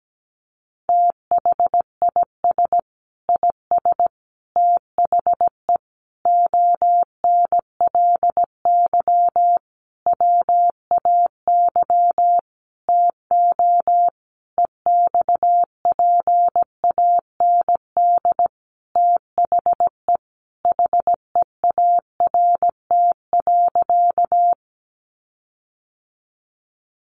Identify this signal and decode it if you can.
Morse code, heard as a keyed tone, about 17 wpm, 700 Hz. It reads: THIS IS THE ONLY WAY TO EXPAND THE HEART.
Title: Sentences from Top 1400 Words 17wpm